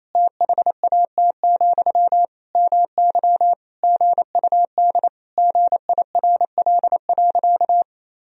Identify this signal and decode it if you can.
Morse code, heard as a keyed tone, about 27 wpm, 700 Hz. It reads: THAT, MY GUD GIRL.
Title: Sentences from Top 400 Words and Abbreviations 28wpm